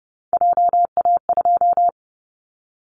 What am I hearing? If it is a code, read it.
Morse code, heard as a keyed tone, about 30 wpm, 700 Hz. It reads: JA2
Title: Callsign Prefixes for Top 25 DXCC - 30wpm